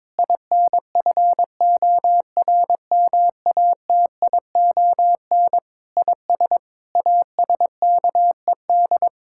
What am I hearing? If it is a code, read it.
Morse code, heard as a keyed tone, about 22 wpm, 700 Hz. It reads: INFORMATION IS ASKED
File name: practice / 22wpm / Sets of 7 Words - 22wpm